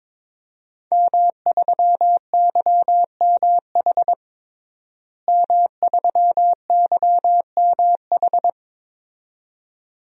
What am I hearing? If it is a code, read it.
Morse code, heard as a keyed tone, about 21 wpm, 700 Hz. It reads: M3YMH M3YMH
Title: DXCC-100 Call Signs - 5 Characters - Repeated 2x - 22wpm